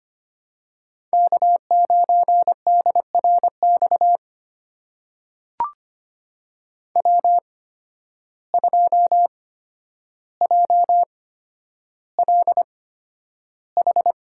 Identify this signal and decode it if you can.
Morse code, heard as a keyed tone, about 25 wpm, spaced as for 11 wpm, 700 Hz. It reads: K9DRX W2JLH
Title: USA Call Signs - 5 Characters - 8x Character Spacing - 25wpm